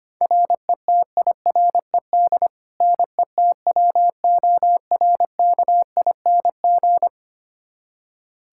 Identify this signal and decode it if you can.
Morse code, heard as a keyed tone, about 25 wpm, 700 Hz. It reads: RETIRED NETWORKING
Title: Practice QSO - Exchange 3 - More Info - 25wpm